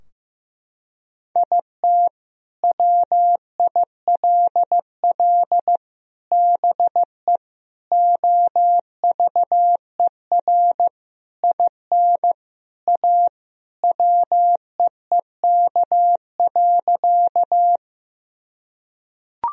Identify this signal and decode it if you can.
Morse code, heard as a keyed tone, about 15 wpm, 700 Hz. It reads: IT WILL BE OVER IN A WEEK.
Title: Sentences from Top 1000 Words - Review Part 2 - 15wpm